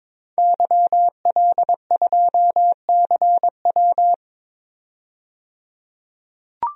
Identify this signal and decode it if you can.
Morse code, heard as a keyed tone, about 22 wpm, 700 Hz. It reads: YL2CW
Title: DXCC-100 Call Signs - 5 Characters - 22wpm